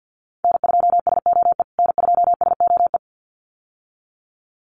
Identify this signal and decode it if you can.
Morse code, heard as a keyed tone, about 50 wpm, 700 Hz. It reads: N2SOE N2SOE
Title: USA Call Signs Repeated 2x 50wpm